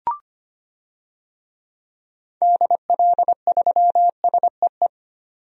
Morse code, heard as a keyed tone, about 25 wpm, 700 Hz. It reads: DL3SEE
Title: DXCC-100 Call Signs - 6 Characters - 25wpm